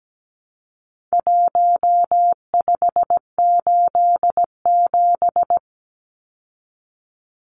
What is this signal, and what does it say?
Morse code, heard as a keyed tone, about 17 wpm, 700 Hz. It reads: 1587
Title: Numbers - Contest Style - 4-Digit 17wpm